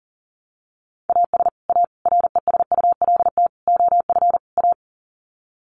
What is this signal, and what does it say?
Morse code, heard as a keyed tone, about 40 wpm, 700 Hz. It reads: AS A RESULT OF A
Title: Sets of 5 Words Encore 40wpm